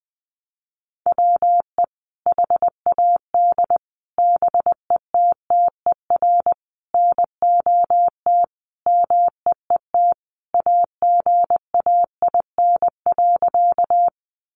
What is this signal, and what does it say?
Morse code, heard as a keyed tone, about 20 wpm, 700 Hz. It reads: WE HAD BETTER NOT MEET AGAIN.
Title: Sentences from Top 1100 Words 20wpm